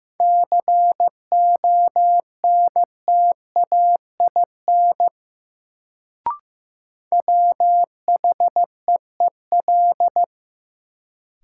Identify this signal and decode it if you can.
Morse code, heard as a keyed tone, about 15 wpm, 700 Hz. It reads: CONTAIN WHEEL
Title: Sentences from Top 500 Words - New Word Drill - 15wpm